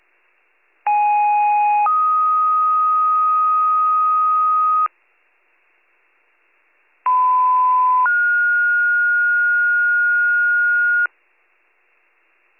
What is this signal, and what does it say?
Two-tone page, heard as a one-second tone, then a three-second tone, about 830 then 1250 Hz, about 1010 then 1430 Hz. It is heard from 0.9 to 11.1 s.